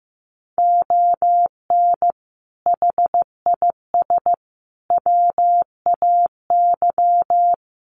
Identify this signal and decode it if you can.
Morse code, heard as a keyed tone, about 15 wpm, 700 Hz. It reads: ON HIS WAY
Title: Sets of 5 Words Encore2 15wpm